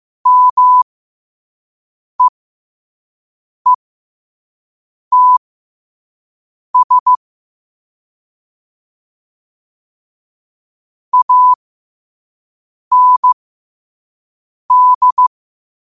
Morse code, heard as a keyed tone, about 15 wpm, spaced as for 5 wpm, 1000 Hz. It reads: MEETS AND
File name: qotdfiles / 5wpm